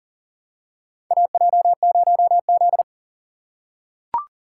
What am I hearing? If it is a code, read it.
Morse code, heard as a keyed tone, about 40 wpm, 700 Hz. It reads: AJ0Z
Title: USA Call Signs - 4 Characters - 3x Character Spacing - 40wpm